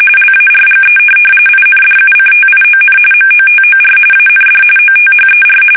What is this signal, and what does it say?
RTTY, about 45.45 baud. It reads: LARGE INDUCTION MOTOR AND OTHER
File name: fsk850 / benjamin-g-lamme